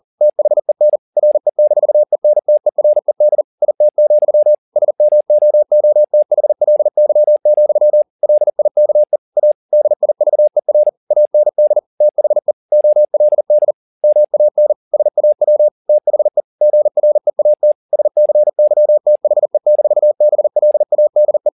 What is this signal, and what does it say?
Morse code, heard as a keyed tone, about 40 wpm, 600 Hz. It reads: THEN RE-ENTERED IT, SMOOTHLY, LIKE A DIVER AND THE OLD MAN SAW THE GREAT SCYTHE-BLADE